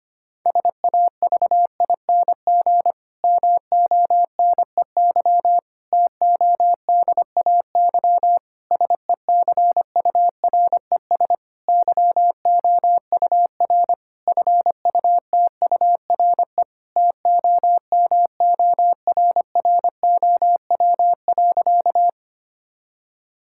Morse code, heard as a keyed tone, about 25 wpm, 700 Hz. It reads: SAVING MONEY TODAY SECURES YOUR FUTURE TOMORROW.
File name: practice / 25wpm / Difficult Sentences with Hints - 25wpm